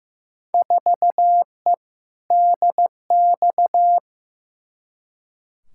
Morse code, heard as a keyed tone, about 15 wpm, 700 Hz. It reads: 4E DX
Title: Field Day Exchanges Encore - 15wpm